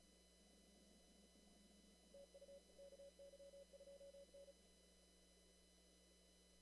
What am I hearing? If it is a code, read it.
Morse code, heard as a keyed tone, about 34 wpm, 560 Hz. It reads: TU KY2N
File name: CW / kt5j wpx cw-op-k5pi